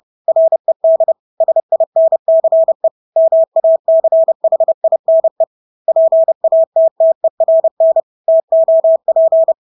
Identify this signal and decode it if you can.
Morse code, heard as a keyed tone, about 30 wpm, 650 Hz. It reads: RED SINCE MACHINE PATTERN TOP
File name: Engelske ord 30-30 wpm